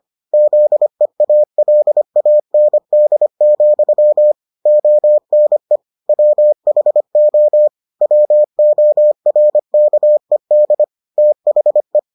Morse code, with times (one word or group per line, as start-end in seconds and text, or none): 0.3-4.3 ZEALAND,
4.7-5.8 ONE
6.1-7.7 WHO
8.0-10.8 WORKED
11.2-12.0 THE